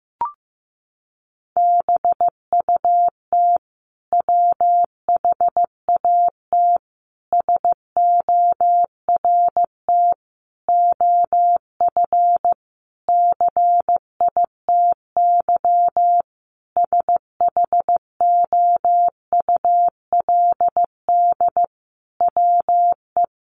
Morse code, heard as a keyed tone, about 15 wpm, 700 Hz. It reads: BUT WHAT SORT OF CITY SHOULD WE
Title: Sentences from Top 1300 Words 15wpm